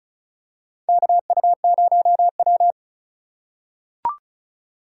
Morse code, heard as a keyed tone, about 34 wpm, 700 Hz. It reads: KU0W
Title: USA Call Signs - 4 Characters - 4x Character Spacing - 35wpm